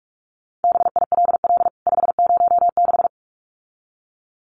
Morse code, heard as a keyed tone, about 44 wpm, 700 Hz. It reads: BILL 506
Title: CWT Exchange 45wpm